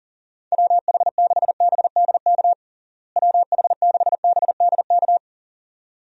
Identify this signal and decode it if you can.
Morse code, heard as a keyed tone, about 40 wpm, 700 Hz. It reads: WH6BDK WH6BDK
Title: USA Call Signs - 6 Characters - Repeated 2x - 40wpm